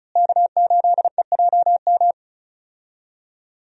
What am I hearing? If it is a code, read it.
Morse code, heard as a keyed tone, about 36 wpm, 700 Hz. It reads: K8EJM
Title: USA Call Signs - 5 Characters - Repeated 2x - 35wpm